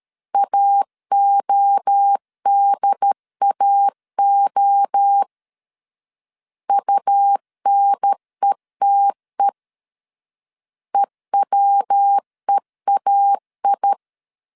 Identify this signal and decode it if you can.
Morse code, heard as a keyed tone, about 13 wpm, 800 Hz. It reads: AODAO UNETE EWEAI
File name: CW / DET / 1 3 1500C S B 12wpm 2828